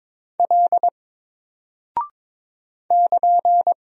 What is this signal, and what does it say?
Morse code, heard as a keyed tone, about 22 wpm, 700 Hz. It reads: L <KN>
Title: Single Character (Letter:Number:Punctuation:Prosign) Alt 22wpm